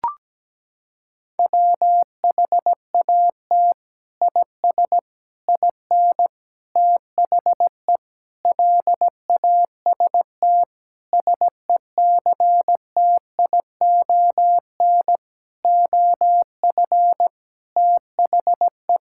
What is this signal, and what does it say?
Morse code, heard as a keyed tone, about 17 wpm, 700 Hz. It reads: WHAT IS IN THE LAST SECTION OF THE
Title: Sentences from Top 1000 Words - Review Part 5 - 17wpm